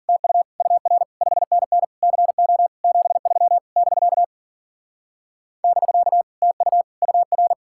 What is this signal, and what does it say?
Morse code, heard as a keyed tone, about 52 wpm, 700 Hz. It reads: TU UR 5NN CO 73 <BK> <BK> TU UR